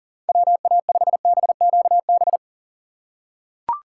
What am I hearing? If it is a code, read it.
Morse code, heard as a keyed tone, about 40 wpm, 700 Hz. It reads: WA5BQB
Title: USA Call Signs - 6 Characters - 3x Character Spacing - 40wpm